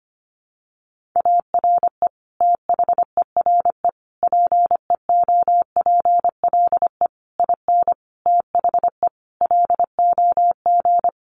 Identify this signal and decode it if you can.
Morse code, heard as a keyed tone, about 25 wpm, 700 Hz. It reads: ARE THERE PEOPLE IN THE LOG
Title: Sentences from Top 1400 Words 25wpm